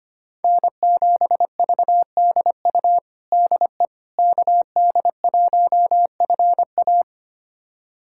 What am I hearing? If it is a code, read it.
Morse code, heard as a keyed tone, about 25 wpm, 700 Hz. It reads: N7VDU DE KD1FA
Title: USA Callsign DE Callsign - 5 Characters - 25wpm